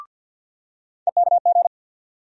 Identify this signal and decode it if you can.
Morse code, heard as a keyed tone, about 47 wpm, 700 Hz. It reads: EKG
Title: USA Call Sign Postfixes - 3 Letters 50wpm